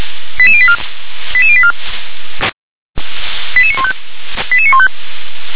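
Five-tone selective call (ZVEI), heard as quick runs of five tones, tones of about 70 ms. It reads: 88084 88084 88915 88915
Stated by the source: Received by SDR as a ZVEI five-tone selective call.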